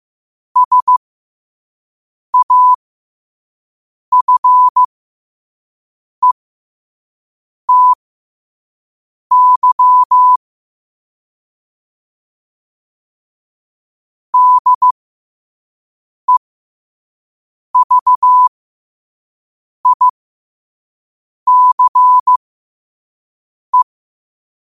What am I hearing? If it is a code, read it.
Morse code, heard as a keyed tone, about 15 wpm, spaced as for 5 wpm, 1000 Hz. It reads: SAFETY DEVICE
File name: qotdfiles / 5wpm